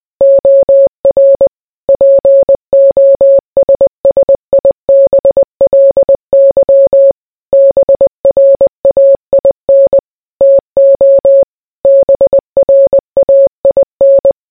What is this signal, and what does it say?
Morse code, heard as a keyed tone, about 20 wpm, 550 Hz. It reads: OR POSSIBLY BRAIN TO BRAIN